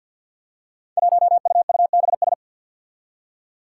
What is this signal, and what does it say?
Morse code, heard as a keyed tone, about 46 wpm, 700 Hz. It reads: 1UUBS
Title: Five Challenging Letter-Number Combinations 50wpm